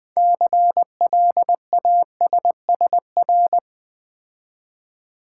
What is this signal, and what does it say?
Morse code, heard as a keyed tone, about 20 wpm, 700 Hz. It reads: CLASSR
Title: Build A Word - 3 Letters At A Time - 9-Letter Words - 20wpm